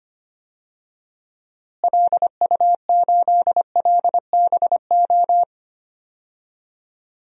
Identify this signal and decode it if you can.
Morse code, heard as a keyed tone, about 25 wpm, 700 Hz. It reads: LU8LBO
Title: DXCC-100 Call Signs - 6 Characters - Repeated 2x - 25wpm